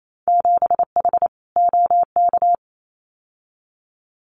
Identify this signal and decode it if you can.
Morse code, heard as a keyed tone, about 28 wpm, 700 Hz. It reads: 7H OK
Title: Winter Field Day Exchanges Encore 28wpm